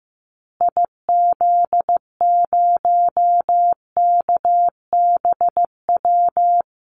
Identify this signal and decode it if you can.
Morse code, heard as a keyed tone, about 14 wpm, 700 Hz. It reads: IZ0KBW
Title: Call Signs Repeated 2x - 15wpm